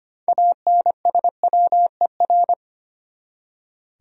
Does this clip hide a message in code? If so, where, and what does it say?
Yes: Morse code from 0.3 to 2.5 s, heard as a keyed tone, about 33 wpm, 700 Hz. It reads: ANSWER